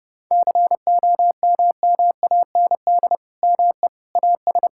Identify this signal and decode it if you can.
Morse code, heard as a keyed tone, about 30 wpm, 700 Hz. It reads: COMMAND ME AS
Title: Sentences from Top 1600 Words 30wpm